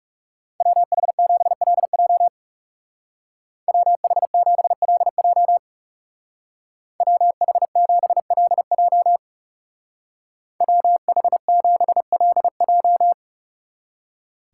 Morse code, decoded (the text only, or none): WH7LJ WH7LJ WH7LJ WH7LJ